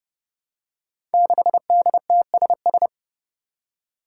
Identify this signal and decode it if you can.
Morse code, heard as a keyed tone, about 30 wpm, 700 Hz. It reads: 6DTSS
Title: Five Challenging Letter-Number Combinations 30wpm